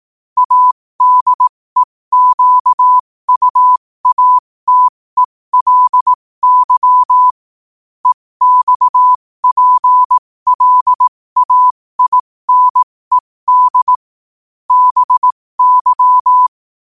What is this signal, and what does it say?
Morse code, heard as a keyed tone, about 18 wpm, spaced as for 15 wpm, 1000 Hz. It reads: ADEQUATELY EXPLAINED BY
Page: New quotes every day in morse code at 15 Words per minute.